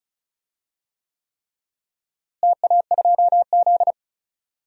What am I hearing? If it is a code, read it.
Morse code, heard as a keyed tone, about 34 wpm, 700 Hz. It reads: TA2Z
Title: DXCC-100 Call Signs - 4 Characters - Repeated 3x - 35wpm